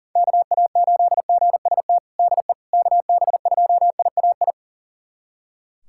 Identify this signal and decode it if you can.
Morse code, heard as a keyed tone, about 40 wpm, 700 Hz. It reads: KA8GST DE KB2IAI
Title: USA Callsign DE Callsign - 6 Characters - 40wpm